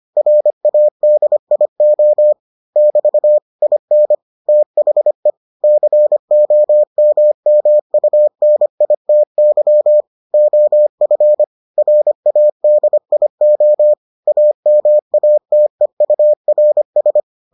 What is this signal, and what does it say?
Morse code, heard as a keyed tone, about 25 wpm, 600 Hz. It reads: RADIO <BT> IN THE COMMUNITY OF RADIO AMATEURS